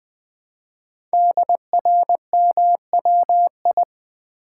Morse code, heard as a keyed tone, about 20 wpm, 700 Hz. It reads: DRMWI
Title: Code Groups - 5 Letters - 20wpm